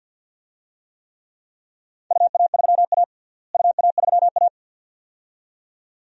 Morse code, heard as a keyed tone, about 51 wpm, 700 Hz. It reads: UA3A UA3A
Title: DXCC-100 Call Signs - 4 Characters - Repeated 2x - 50wpm